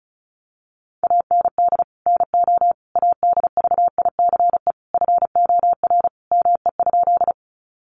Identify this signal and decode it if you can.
Morse code, heard as a keyed tone, about 34 wpm, 700 Hz. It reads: AND NO ADVICE FOR ME?
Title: Sentences from Top 1100 Words 35wpm